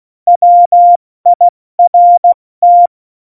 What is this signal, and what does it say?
Morse code, heard as a keyed tone, about 16 wpm, 700 Hz. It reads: WIRT
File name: Morsekurs / Witze / Witz 22. Dezember 2020, 70 F80